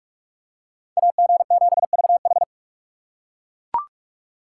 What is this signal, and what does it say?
Morse code, heard as a keyed tone, about 40 wpm, 700 Hz. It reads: AG7VH